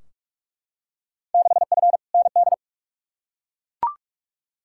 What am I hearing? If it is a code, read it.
Morse code, heard as a keyed tone, about 42 wpm, 700 Hz. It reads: 6F ND
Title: Field Day Exchanges Encore - 45wpm